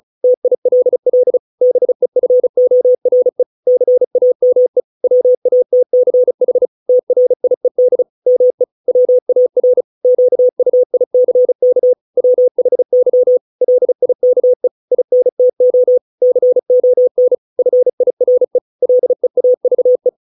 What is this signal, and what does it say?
Morse code, heard as a keyed tone, about 34 wpm, 500 Hz. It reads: TILL BEFORE CAME WATCH TRIED ME WAR QUICK WHY LIKE INTO CON FIRE LEAVE